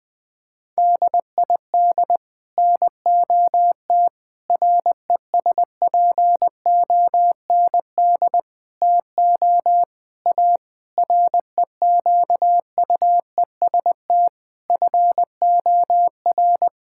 Morse code, heard as a keyed tone, about 20 wpm, 700 Hz. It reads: DID NOT RESPOND TO A REQUEST FOR
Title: Sets of 8 Words - 20wpm